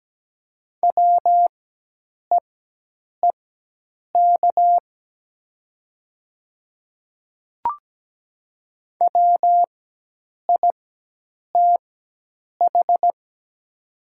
Morse code, heard as a keyed tone, about 17 wpm, spaced as for 8 wpm, 700 Hz. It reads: WEEK WITH
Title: 4x Character Spacing - Top 500 Words - 17wpm